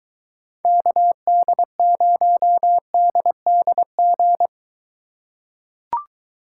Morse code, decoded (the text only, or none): KD0DDG